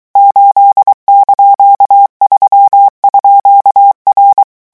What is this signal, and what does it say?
Morse code, heard as a keyed tone, about 23 wpm, 800 Hz. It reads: 8)3_R
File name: morse mp3 / revision19-vitesse 22 mots